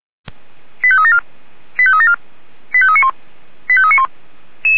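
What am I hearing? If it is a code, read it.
Five-tone selective call (ZVEI), heard as quick runs of five tones, tones of about 70 ms. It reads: 85374 85374 85391 85391
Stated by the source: ZVEI five-tone call received by SDR, recorded by the Orlamünde fire brigade (Feuerwehr Orlamünde).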